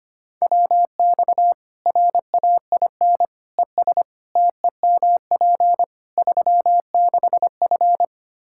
Morse code, heard as a keyed tone, about 25 wpm, 700 Hz. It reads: WX RAIN ES TEMP 36F
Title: Practice QSO - Exchange 2 - Info - 25wpm